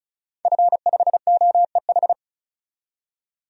Morse code, heard as a keyed tone, about 36 wpm, 700 Hz. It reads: F5OEH
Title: DXCC-100 Call Signs - 5 Characters - 35wpm